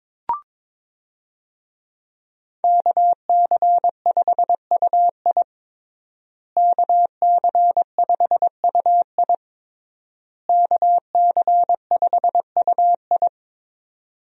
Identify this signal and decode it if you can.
Morse code, heard as a keyed tone, about 22 wpm, 700 Hz. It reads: KC5UI KC5UI KC5UI
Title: USA Call Signs - 5 Characters - Repeated 3x - 22wpm